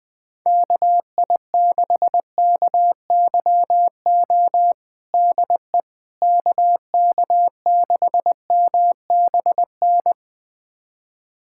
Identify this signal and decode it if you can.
Morse code, heard as a keyed tone, about 20 wpm, 700 Hz. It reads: KI6KYO DE KK6MBN
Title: USA Callsign DE Callsign - 6 Characters - 20wpm